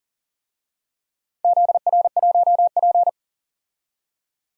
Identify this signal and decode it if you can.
Morse code, heard as a keyed tone, about 38 wpm, 700 Hz. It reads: ZR1P
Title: DXCC-100 Call Signs - 4 Characters - 40wpm